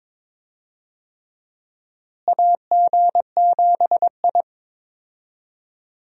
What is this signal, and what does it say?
Morse code, heard as a keyed tone, about 22 wpm, 700 Hz. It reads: AG7I
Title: USA Call Signs - 4 Characters - Repeated 3x - 22wpm